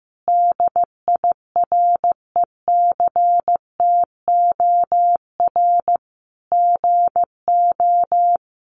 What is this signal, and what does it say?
Morse code, heard as a keyed tone, about 15 wpm, 700 Hz. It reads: DIRECTOR GO
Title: Sentences from Top 1300 Words 15wpm